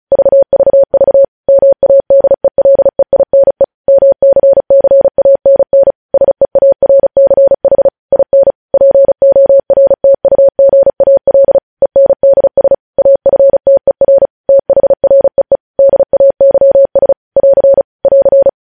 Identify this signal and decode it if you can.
Morse code, heard as a keyed tone, about 36 wpm, 550 Hz. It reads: VVV MADELEINE MCCANN SEARCH IN PORTUGAL ENDS AFTER THREE DAYS <AR> <AR>
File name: podcast / News headlines-1x-35x35